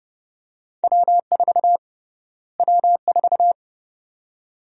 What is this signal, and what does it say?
Morse code, heard as a keyed tone, about 31 wpm, 700 Hz. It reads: W4 W4
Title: USA Call Sign Prefixes - Repeated 2x - 30wpm